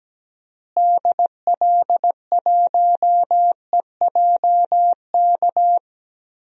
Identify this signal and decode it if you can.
Morse code, heard as a keyed tone, about 17 wpm, 700 Hz. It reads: DL1EJK
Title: DXCC-100 Call Signs - 6 Characters - Repeated 3x - 17wpm